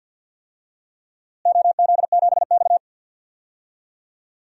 Morse code, heard as a keyed tone, about 53 wpm, 700 Hz. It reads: OZ7X
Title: DXCC-100 Call Signs - 4 Characters - Repeated 3x - 50wpm